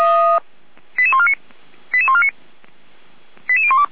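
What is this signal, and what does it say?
Five-tone selective call (ZVEI), heard as quick runs of five tones, tones of about 70 ms. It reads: 88149 88149 88012
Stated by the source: ZVEI five-tone selective call received on SDR.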